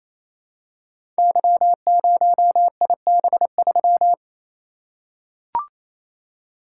Morse code, heard as a keyed tone, about 28 wpm, 700 Hz. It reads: Y0IB3
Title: Code Groups - 5 Letter:Numbers - 28wpm